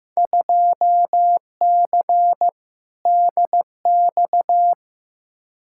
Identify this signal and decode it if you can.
Morse code, heard as a keyed tone, about 15 wpm, 700 Hz. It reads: 2C DX
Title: Field Day Exchanges Encore - 15wpm